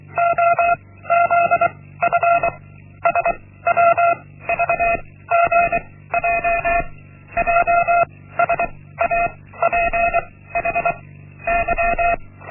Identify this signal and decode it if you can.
Morse code, heard as a keyed tone, about 23 wpm, 690 Hz. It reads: O Z F S W V G J J S A P H Y
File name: Linksmakalnis-1